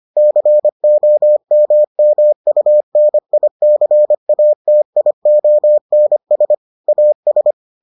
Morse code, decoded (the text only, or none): COMMUNICATIONS AS